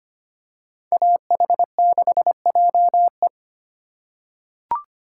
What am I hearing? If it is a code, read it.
Morse code, heard as a keyed tone, about 25 wpm, 700 Hz. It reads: AH6JE
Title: USA Call Signs - 5 Characters - 4x Character Spacing - 25wpm